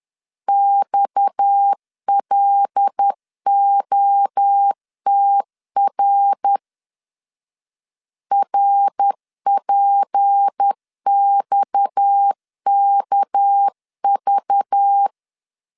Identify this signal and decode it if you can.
Morse code, heard as a keyed tone, about 10 wpm, 800 Hz. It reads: XLOTR RPXKV
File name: LEZIONI / 6 PKX0 / 6 PKX 10wpm 2426